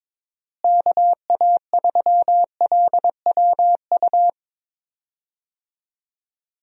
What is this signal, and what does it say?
Morse code, heard as a keyed tone, about 23 wpm, 700 Hz. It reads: KA3LWU
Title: USA Call Signs - 6 Characters - Repeated 2x - 22wpm